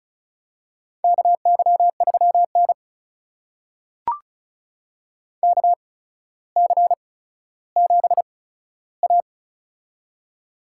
Morse code, heard as a keyed tone, about 36 wpm, spaced as for 15 wpm, 700 Hz. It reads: KY3N KC7A
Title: USA Call Signs - 4 Characters - 8x Character Spacing - 35wpm